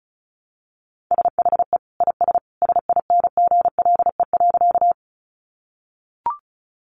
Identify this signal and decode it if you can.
Morse code, heard as a keyed tone, about 34 wpm, 700 Hz. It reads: SHE IS SINGLE.
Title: Sentences from Top 700 Words 35wpm